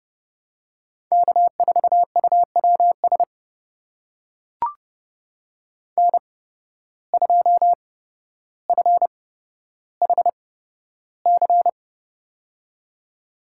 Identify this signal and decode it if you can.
Morse code, heard as a keyed tone, about 30 wpm, spaced as for 13 wpm, 700 Hz. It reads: K4UWS N2FHC
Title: USA Call Signs - 5 Characters - 8x Character Spacing - 30wpm